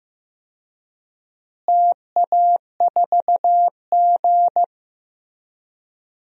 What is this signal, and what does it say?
Morse code, heard as a keyed tone, about 15 wpm, 700 Hz. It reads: TA4G
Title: DXCC-100 Call Signs - 4 Characters - Repeated 2x - 15wpm